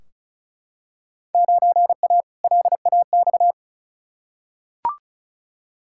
Morse code, heard as a keyed tone, about 34 wpm, 700 Hz. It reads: 9A LAX
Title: Field Day Exchanges Encore - 35wpm